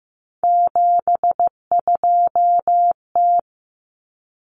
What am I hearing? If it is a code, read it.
Morse code, heard as a keyed tone, about 15 wpm, 700 Hz. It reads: 72T
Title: Common Cut Numbers - 3-Digit 15wpm